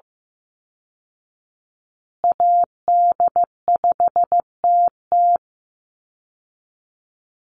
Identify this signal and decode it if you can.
Morse code, heard as a keyed tone, about 15 wpm, 700 Hz. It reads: AD5TT
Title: Call Signs Repeated 3x - 15wpm